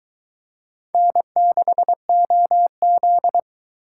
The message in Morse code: N6OZ

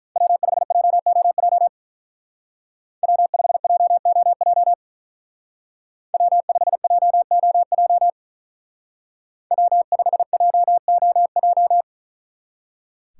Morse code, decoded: W5JOJ W5JOJ W5JOJ W5JOJ

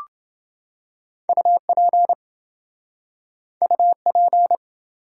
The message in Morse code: UP UP

